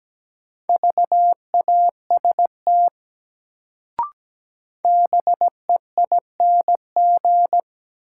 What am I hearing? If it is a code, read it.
Morse code, heard as a keyed tone, about 17 wpm, 700 Hz. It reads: VAST BEING
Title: Sets of 4 Words - New Word Drill - 17wpm